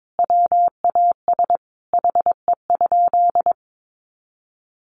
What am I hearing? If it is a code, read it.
Morse code, heard as a keyed tone, about 22 wpm, 700 Hz. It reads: WAS HE?
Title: Sentences from Top 100 Words 22wpm